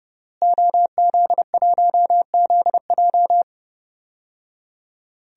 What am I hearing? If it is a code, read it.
Morse code, heard as a keyed tone, about 30 wpm, 700 Hz. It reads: OZ1ZJ
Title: DXCC-100 Call Signs - 5 Characters - Repeated 3x - 30wpm